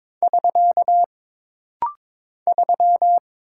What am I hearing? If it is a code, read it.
Morse code, heard as a keyed tone, about 23 wpm, 700 Hz. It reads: <SK> 3
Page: This rapid-fire timing is geared to push your ICR (Instant Character Recognition) skills.